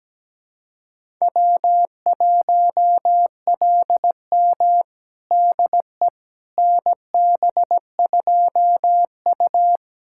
Morse code, heard as a keyed tone, about 17 wpm, 700 Hz. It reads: W1LM DE NB2U
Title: USA Callsign DE Callsign - 4 Characters - 17wpm